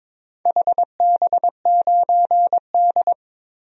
Morse code, heard as a keyed tone, about 22 wpm, 700 Hz. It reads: HB9D
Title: DXCC-100 Call Signs - 4 Characters - 22wpm